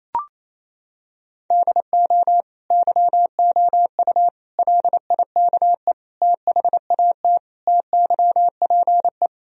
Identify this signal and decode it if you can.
Morse code, heard as a keyed tone, about 28 wpm, 700 Hz. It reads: DO YOU LIKE THAT TYPE
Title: Sentences from Top 1000 Words - Review Part 3 - 28wpm